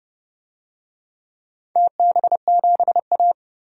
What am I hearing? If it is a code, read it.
Morse code, heard as a keyed tone, about 30 wpm, 700 Hz. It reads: TB7A
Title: DXCC-100 Call Signs - 4 Characters - 30wpm